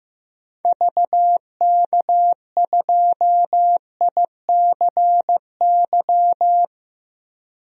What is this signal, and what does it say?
Morse code, heard as a keyed tone, about 16 wpm, 700 Hz. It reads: VK2ICY
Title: DXCC-100 Call Signs - 6 Characters - 15wpm